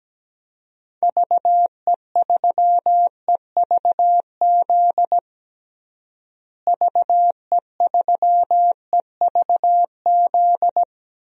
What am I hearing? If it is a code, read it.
Morse code, heard as a keyed tone, about 17 wpm, 700 Hz. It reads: VE3EVZ VE3EVZ
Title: DXCC-100 Call Signs - 6 Characters - Repeated 3x - 17wpm